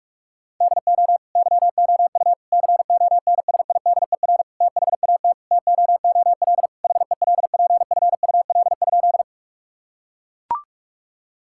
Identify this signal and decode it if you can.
Morse code, heard as a keyed tone, about 41 wpm, 700 Hz. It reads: DO YOU CONSIDER THAT TOOL HELPFUL?
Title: Sentences from Top 1000 Words - Review Part 6 - 45wpm